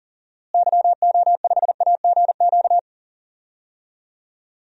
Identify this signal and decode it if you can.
Morse code, heard as a keyed tone, about 40 wpm, 700 Hz. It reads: YO5AGQ
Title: DXCC-100 Call Signs - 6 Characters - Repeated 2x - 40wpm